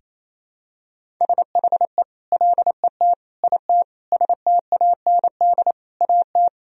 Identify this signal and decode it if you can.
Morse code, heard as a keyed tone, about 28 wpm, 700 Hz. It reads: SHE LET IT STAND AT
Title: Sentences from Top 1000 Words - Review Part 1 - 28wpm